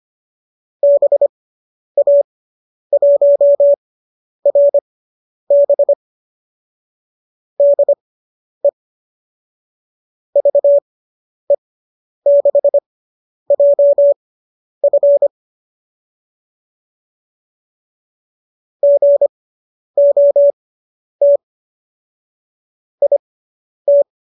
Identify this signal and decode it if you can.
Morse code, heard as a keyed tone, about 25 wpm, spaced as for 10 wpm, 570 Hz. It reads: BA1RB DE VE6JF GOT IT